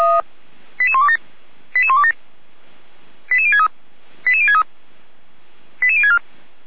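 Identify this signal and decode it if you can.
Five-tone selective call (ZVEI), heard as quick runs of five tones, tones of about 70 ms. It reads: 88127 88127 88063 88063 88064
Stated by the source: ZVEI five-tone selective call 88125, received via SDR.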